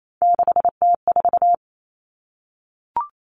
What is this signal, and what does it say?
Morse code, heard as a keyed tone, about 28 wpm, 700 Hz. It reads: 6T4
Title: Common Cut Numbers - 3-Digit 28wpm